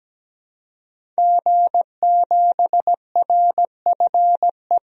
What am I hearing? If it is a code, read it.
Morse code, heard as a keyed tone, about 17 wpm, 700 Hz. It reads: G7RFE
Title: DXCC-100 Call Signs - 5 Characters - 17wpm